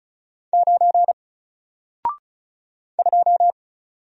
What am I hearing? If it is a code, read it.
Morse code, heard as a keyed tone, about 36 wpm, 700 Hz. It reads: 9 2
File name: Single-Letter-Number-35wpm